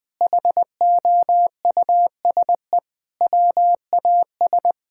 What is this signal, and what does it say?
Morse code, heard as a keyed tone, about 21 wpm, 700 Hz. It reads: HOUSE WAS
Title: Sentences with 3-Word Phrases - 20wpm